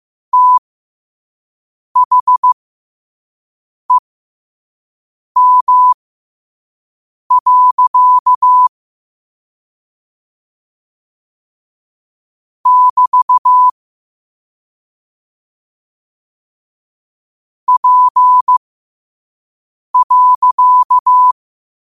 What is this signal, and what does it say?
Morse code, heard as a keyed tone, about 15 wpm, spaced as for 5 wpm, 1000 Hz. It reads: THEM. <BT> P.
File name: qotdfiles / 5wpm